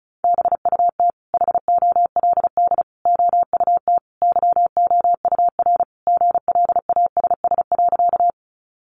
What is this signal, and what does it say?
Morse code, heard as a keyed tone, about 34 wpm, 700 Hz. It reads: BUT HOLD OUT YOUR GLASS.
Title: Sentences from Top 1300 Words 35wpm